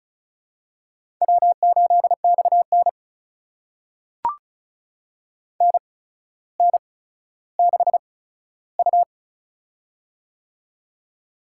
Morse code, heard as a keyed tone, about 34 wpm, spaced as for 15 wpm, 700 Hz. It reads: W8XN NN6U
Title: USA Call Signs - 4 Characters - 8x Character Spacing - 35wpm